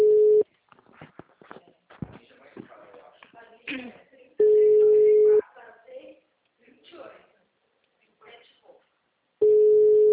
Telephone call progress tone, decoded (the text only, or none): ringback tone